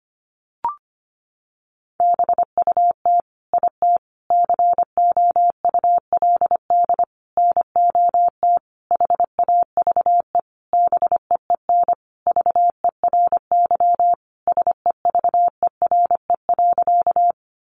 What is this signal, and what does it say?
Morse code, heard as a keyed tone, about 25 wpm, 700 Hz. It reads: BUT IT COULD NOT HAVE BEEN VERY SEVERE.
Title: Sentences from Top 1400 Words 25wpm